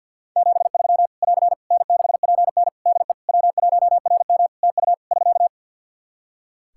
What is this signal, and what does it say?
Morse code, heard as a keyed tone, about 51 wpm, 700 Hz. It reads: 73 <AR> N6PN DE W1RM TU <SK>
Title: Practice QSO - Exchange 4 - Ending - 50wpm